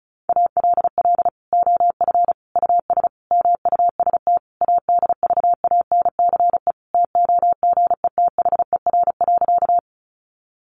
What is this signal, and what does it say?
Morse code, heard as a keyed tone, about 34 wpm, 700 Hz. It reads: ALL OF US MUST ADVANCE TOGETHER.
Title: Sentences from Top 1400 Words 35wpm